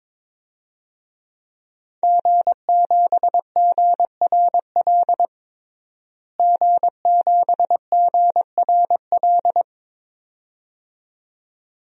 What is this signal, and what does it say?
Morse code, heard as a keyed tone, about 22 wpm, 700 Hz. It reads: G7GRL G7GRL
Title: DXCC-100 Call Signs - 5 Characters - Repeated 2x - 22wpm